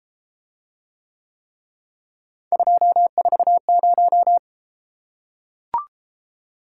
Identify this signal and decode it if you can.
Morse code, heard as a keyed tone, about 33 wpm, 700 Hz. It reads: 240